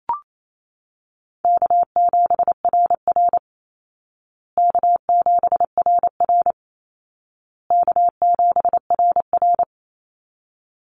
Morse code, heard as a keyed tone, about 28 wpm, 700 Hz. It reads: K7RR K7RR K7RR
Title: Easy Call Signs Repeated 3x - 28wpm